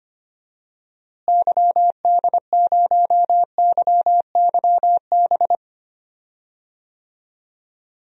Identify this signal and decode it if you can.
Morse code, heard as a keyed tone, about 25 wpm, 700 Hz. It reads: YD0YYB
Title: DXCC-100 Call Signs - 6 Characters - Repeated 3x - 25wpm